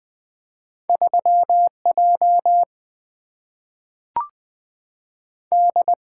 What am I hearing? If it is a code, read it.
Morse code, heard as a keyed tone, about 20 wpm, 700 Hz. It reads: 3 J D
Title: Single-Letter Single-Number 20wpm